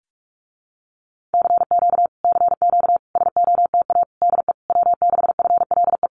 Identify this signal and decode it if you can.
Morse code, heard as a keyed tone, about 46 wpm, 700 Hz. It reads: CQ CQ SOTA DE W6FLE
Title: Common QSO Phrases Encore 45wpm